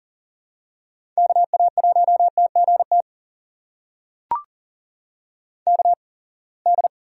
Morse code, heard as a keyed tone, about 40 wpm, spaced as for 28 wpm, 700 Hz. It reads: KA1TGT KD
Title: USA Call Signs - 6 Characters - 8x Character Spacing - 40wpm